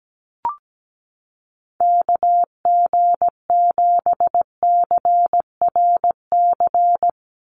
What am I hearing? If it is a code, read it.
Morse code, heard as a keyed tone, about 17 wpm, 700 Hz. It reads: KG7CRC
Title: USA Call Signs 17wpm